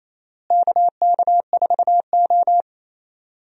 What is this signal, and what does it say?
Morse code, heard as a keyed tone, about 28 wpm, 700 Hz. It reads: KK4O
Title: USA Call Signs - 4 Characters - Repeated 2x - 28wpm